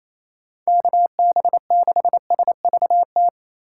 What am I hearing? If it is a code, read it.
Morse code, heard as a keyed tone, about 28 wpm, 700 Hz. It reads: KB6SVT
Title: USA Call Signs - 6 Characters - 2x Character Spacing - 28wpm